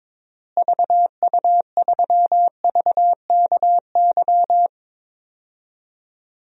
Morse code, heard as a keyed tone, about 22 wpm, 700 Hz. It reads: VU3VKY
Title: DXCC-100 Call Signs - 6 Characters - Repeated 3x - 22wpm